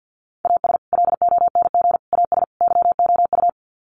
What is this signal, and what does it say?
Morse code, heard as a keyed tone, about 50 wpm, 700 Hz. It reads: AS LONG AS YOU
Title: Sets of 4 Words 50wpm